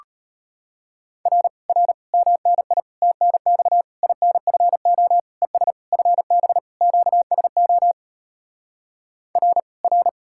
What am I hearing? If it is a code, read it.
Morse code, heard as a keyed tone, about 33 wpm, 700 Hz. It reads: R R MNI TNX INFO ES FB QSO R R